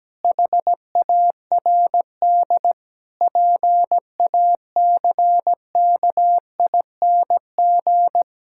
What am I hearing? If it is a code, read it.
Morse code, heard as a keyed tone, about 17 wpm, 700 Hz. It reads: HARD PACKING
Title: Pangram Challenge v2 - 17wpm